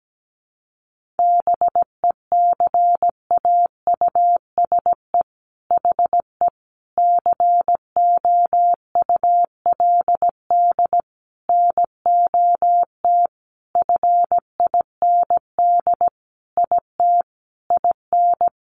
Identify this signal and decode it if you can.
Morse code, heard as a keyed tone, about 17 wpm, 700 Hz. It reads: BECAUSE HE COULD NOT FIND IT IN
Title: Sentences from Top 1400 Words 17wpm